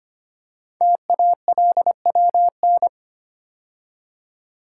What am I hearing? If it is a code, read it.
Morse code, heard as a keyed tone, about 25 wpm, 700 Hz. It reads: TALWN
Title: Code Groups - 5 Letters - 25wpm